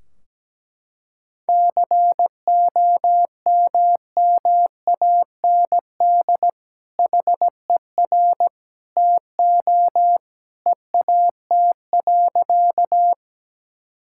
Morse code, heard as a keyed tone, about 17 wpm, 700 Hz. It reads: COMMAND HER TO EAT.
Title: Sentences from Top 1600 Words 17wpm